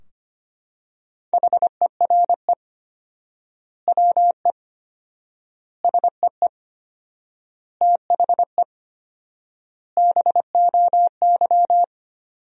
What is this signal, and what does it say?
Morse code, heard as a keyed tone, about 25 wpm, 700 Hz. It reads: HERE WE SEE THE BOY
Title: 4x Word-Spacing - Sentences from Top 300 Words - 25wpm